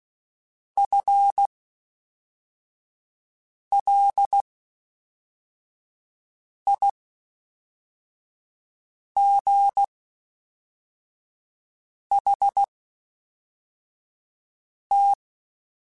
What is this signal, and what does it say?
Morse code, heard as a keyed tone, about 16 wpm, spaced as for 4 wpm, 800 Hz. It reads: FLIGHT
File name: Spaced / FL Runs sp mp3s / FL Runs sp @5